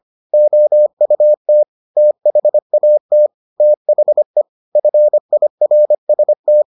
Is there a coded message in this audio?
Yes: Morse code from 0.3 to 6.6 s, heard as a keyed tone, about 25 wpm, 600 Hz. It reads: OUT THAT THE FIRST